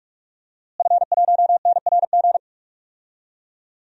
Morse code, heard as a keyed tone, about 41 wpm, 700 Hz. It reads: F1NRG
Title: DXCC-100 Call Signs - 5 Characters - Repeated 3x - 45wpm